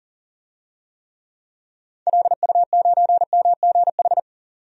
Morse code, heard as a keyed tone, about 40 wpm, 700 Hz. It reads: LU9MGH
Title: DXCC-100 Call Signs - 6 Characters - 40wpm